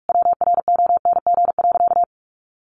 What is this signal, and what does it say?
Morse code, heard as a keyed tone, about 44 wpm, 700 Hz. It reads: WRONG.
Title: Sentences from Top 1300 Words 45wpm